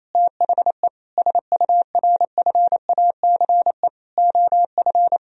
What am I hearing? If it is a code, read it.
Morse code, heard as a keyed tone, about 28 wpm, 700 Hz. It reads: THE SURFACE OF
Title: Sentences from Top 1000 Words - Review Part 3 - 28wpm